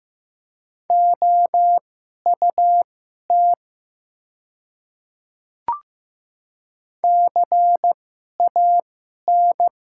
Morse code, heard as a keyed tone, about 15 wpm, spaced as for 11 wpm, 700 Hz. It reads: OUT CAN
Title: 2x Character Spacing - Top 100 Words - 15wpm